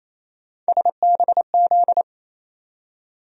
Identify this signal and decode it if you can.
Morse code, heard as a keyed tone, about 28 wpm, 700 Hz. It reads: SBZ
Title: USA Call Sign Postfixes - 3 Letters 28wpm